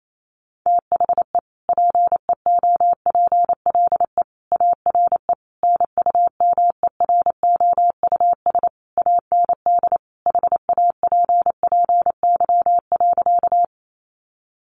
Morse code, heard as a keyed tone, about 28 wpm, 700 Hz. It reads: THE PEOPLE ARE NUMEROUS AND HAPPY.
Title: Sentences from Top 1400 Words 28wpm